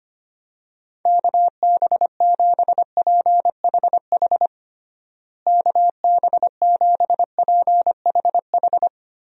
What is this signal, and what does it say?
Morse code, heard as a keyed tone, about 25 wpm, 700 Hz. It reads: KB7PHH KB7PHH
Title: USA Call Signs - 6 Characters - Repeated 3x - 25wpm